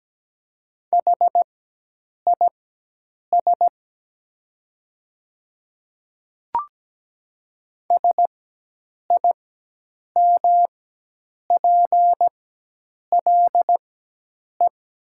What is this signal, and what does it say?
Morse code, heard as a keyed tone, about 17 wpm, spaced as for 8 wpm, 700 Hz. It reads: HIS SIMPLE
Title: 4x Character Spacing - Top 500 Words - 17wpm